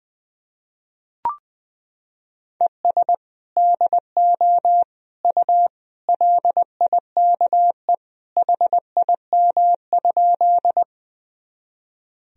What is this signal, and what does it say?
Morse code, heard as a keyed tone, about 21 wpm, 700 Hz. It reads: ES DO U LIKE HIM?
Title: Sentences from Top 100 Words and Abbreviations 20wpm